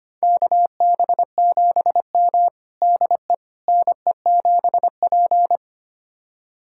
Morse code, heard as a keyed tone, about 25 wpm, 700 Hz. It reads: KB7M DE NE7P
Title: USA Callsign DE Callsign - 4 Characters - 25wpm